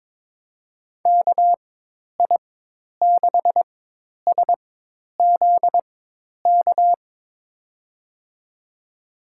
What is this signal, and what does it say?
Morse code, heard as a keyed tone, about 21 wpm, spaced as for 10 wpm, 700 Hz. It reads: KI6SZK
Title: USA Call Signs - 6 Characters - 4x Character Spacing - 22wpm